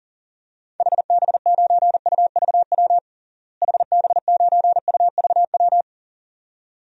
Morse code, heard as a keyed tone, about 40 wpm, 700 Hz. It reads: HB9UVW HB9UVW
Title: DXCC-100 Call Signs - 6 Characters - Repeated 2x - 40wpm